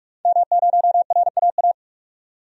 Morse code, heard as a keyed tone, about 41 wpm, 700 Hz. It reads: M0RAA
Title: DXCC-100 Call Signs - 5 Characters - 45wpm